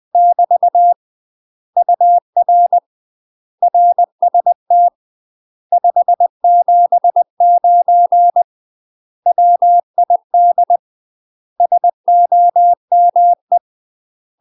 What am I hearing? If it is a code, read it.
Morse code, heard as a keyed tone, about 20 wpm, 700 Hz. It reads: <BT> UR RST 579 WID SOME
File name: UK MASTER QSOs TRAINING / 20WPM / UK MASTER QSOs 0066